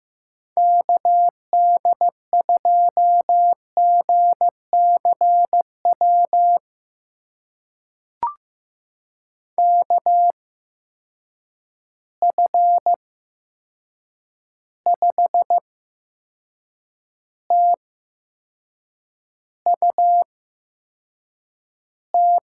Morse code, heard as a keyed tone, about 15 wpm, spaced as for 6 wpm, 700 Hz. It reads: KD2GCW KF5TUT
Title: USA Call Signs - 6 Characters - 8x Character Spacing - 15wpm